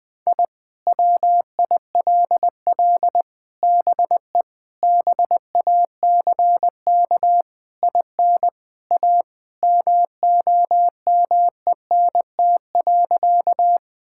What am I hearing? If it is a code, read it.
Morse code, heard as a keyed tone, about 20 wpm, 700 Hz. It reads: I WILL BE BACK IN A MOMENT.
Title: Sentences with 5-Word Phrases - 20wpm